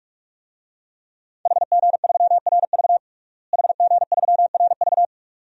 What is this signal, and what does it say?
Morse code, heard as a keyed tone, about 41 wpm, 700 Hz. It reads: HG3RV HG3RV
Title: DXCC-100 Call Signs - 5 Characters - Repeated 2x - 45wpm